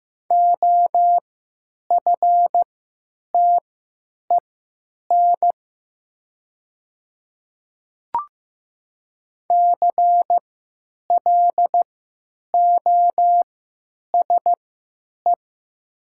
Morse code, heard as a keyed tone, about 15 wpm, spaced as for 9 wpm, 700 Hz. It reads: OFTEN CLOSE
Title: 3x Character Spacing - Top 300 Words - 15wpm